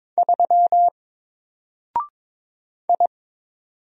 Morse code, heard as a keyed tone, about 22 wpm, 700 Hz. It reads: 3 I
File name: Single-Letter-Number-22wpm